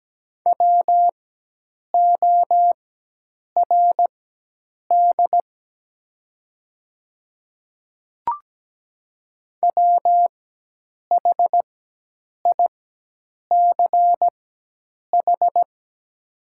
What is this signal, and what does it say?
Morse code, heard as a keyed tone, about 17 wpm, spaced as for 8 wpm, 700 Hz. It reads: WORD WHICH
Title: 4x Character Spacing - Top 100 Words - 17wpm